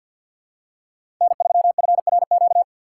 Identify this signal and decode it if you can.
Morse code, heard as a keyed tone, about 47 wpm, 700 Hz. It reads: N3FRQ
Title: USA Call Signs - 5 Characters - 6x Character Spacing - 50wpm